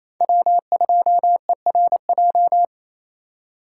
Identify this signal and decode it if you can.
Morse code, heard as a keyed tone, about 32 wpm, 700 Hz. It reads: W2ERJ
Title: Speed Racing - USA 1x3 Call Signs 28wpm